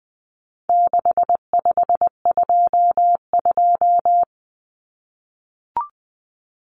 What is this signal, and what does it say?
Morse code, heard as a keyed tone, about 20 wpm, 700 Hz. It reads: 6522
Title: Common Cut Numbers - 4-Digit 20wpm